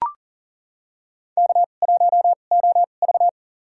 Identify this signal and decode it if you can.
Morse code, heard as a keyed tone, about 40 wpm, spaced as for 29 wpm, 700 Hz. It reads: K1OV
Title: USA Call Signs - 4 Characters - 2x Character Spacing - 40wpm